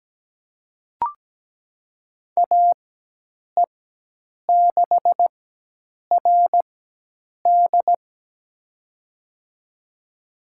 Morse code, heard as a keyed tone, about 17 wpm, spaced as for 8 wpm, 700 Hz. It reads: AE6RD
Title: USA Call Signs - 5 Characters - 4x Character Spacing - 17wpm